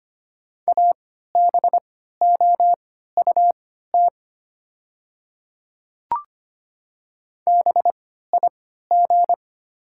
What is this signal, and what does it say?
Morse code, heard as a keyed tone, about 25 wpm, spaced as for 14 wpm, 700 Hz. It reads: ABOUT BIG
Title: 3x Character Spacing - Top 200 Words - 25wpm